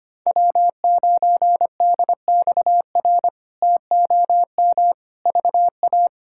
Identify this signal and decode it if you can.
Morse code, heard as a keyed tone, about 25 wpm, 700 Hz. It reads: W9DXR TOM VA
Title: K1USN SST QSO Practice - 25wpm